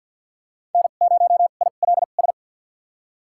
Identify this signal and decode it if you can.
Morse code, heard as a keyed tone, about 52 wpm, spaced as for 37 wpm, 700 Hz. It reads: N0ILS
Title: USA Call Signs - 2x Character Spacing - 50wpm